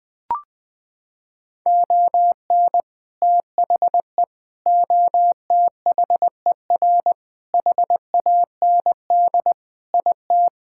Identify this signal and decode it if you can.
Morse code, heard as a keyed tone, about 20 wpm, 700 Hz. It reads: ON THE OTHER HAND IT
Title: Sentences with 4-Word Phrases - 20wpm